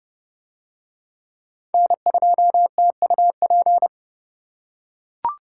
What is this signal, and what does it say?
Morse code, heard as a keyed tone, about 29 wpm, 700 Hz. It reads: N2TUP